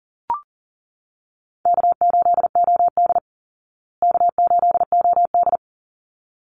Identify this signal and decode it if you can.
Morse code, heard as a keyed tone, about 40 wpm, 700 Hz. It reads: K8OD K8OD
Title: Easy Call Signs Repeated 2x - 40wpm